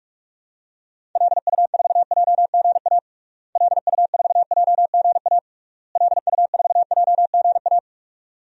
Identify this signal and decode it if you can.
Morse code, heard as a keyed tone, about 41 wpm, 700 Hz. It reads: LU4JGA LU4JGA LU4JGA
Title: DXCC-100 Call Signs - 6 Characters - Repeated 3x - 45wpm